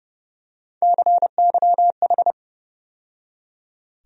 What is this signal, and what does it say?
Morse code, heard as a keyed tone, about 32 wpm, 700 Hz. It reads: CYH
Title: USA Call Sign Postfixes - 3 Letters 30wpm